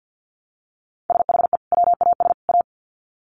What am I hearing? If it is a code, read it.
Morse code, heard as a keyed tone, about 50 wpm, 700 Hz. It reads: SHE WAS A
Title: Sets of 3 Words 50wpm